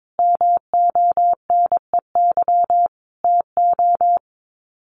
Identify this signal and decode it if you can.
Morse code, heard as a keyed tone, about 22 wpm, 700 Hz. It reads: MONEY TO
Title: Sets of 5 Words Encore2 22wpm